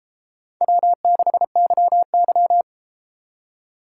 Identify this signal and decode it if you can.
Morse code, heard as a keyed tone, about 32 wpm, 700 Hz. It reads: W6YY